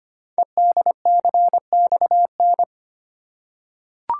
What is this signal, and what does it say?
Morse code, heard as a keyed tone, about 25 wpm, 690 Hz. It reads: EDCXN
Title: Code Groups - 5 Letters - 25wpm